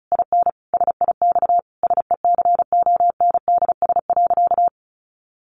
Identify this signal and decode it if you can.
Morse code, heard as a keyed tone, about 34 wpm, 700 Hz. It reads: IN SIX SECONDS.
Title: A Sound Of Thunder 35wpm